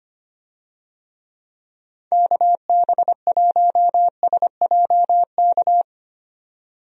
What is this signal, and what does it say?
Morse code, heard as a keyed tone, about 25 wpm, 700 Hz. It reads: KB1SJK
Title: USA Call Signs - 6 Characters - 25wpm